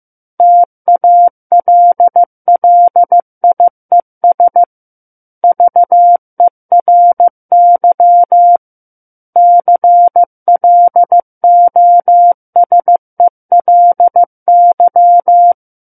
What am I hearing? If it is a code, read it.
Morse code, heard as a keyed tone, about 15 wpm, 700 Hz. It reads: TALLIES VERY CLOSELY
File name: Chapter 6 - 15 WPM